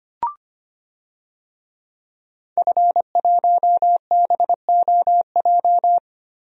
Morse code, heard as a keyed tone, about 25 wpm, 700 Hz. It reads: F1BOJ
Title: DXCC-100 Call Signs - 5 Characters - Repeated 3x - 25wpm